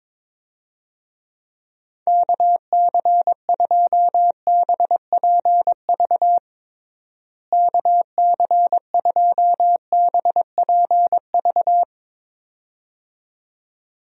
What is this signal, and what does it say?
Morse code, heard as a keyed tone, about 22 wpm, 700 Hz. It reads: KC2BPV KC2BPV
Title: USA Call Signs - 6 Characters - Repeated 2x - 22wpm